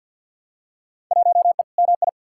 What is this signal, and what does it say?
Morse code, heard as a keyed tone, about 48 wpm, 700 Hz. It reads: 1E MI
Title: Field Day Exchanges - 50wpm